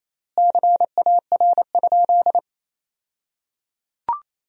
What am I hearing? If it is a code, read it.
Morse code, heard as a keyed tone, about 28 wpm, 700 Hz. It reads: CAR?
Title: 2x Word-Spacing - Sentences from Top 300 Words - 28wpm